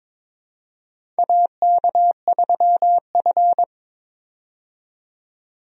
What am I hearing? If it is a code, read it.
Morse code, heard as a keyed tone, about 22 wpm, 700 Hz. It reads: AK3F
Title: USA Call Signs - 4 Characters - 22wpm